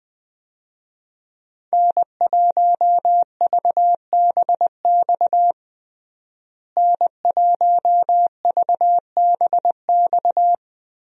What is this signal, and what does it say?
Morse code, heard as a keyed tone, about 20 wpm, 700 Hz. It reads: N1VBX N1VBX
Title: USA Call Signs - 5 Characters - Repeated 2x - 20wpm